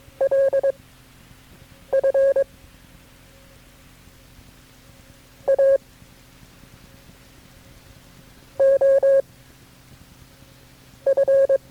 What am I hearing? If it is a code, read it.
Morse code, heard as a keyed tone, about 22 wpm, spaced as for 6 wpm, 540 Hz. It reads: LF A O F
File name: KB6MT-tape3-side1